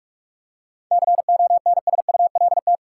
Morse code, heard as a keyed tone, about 46 wpm, 700 Hz. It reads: CONSULT
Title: Seven Letter Words - 45wpm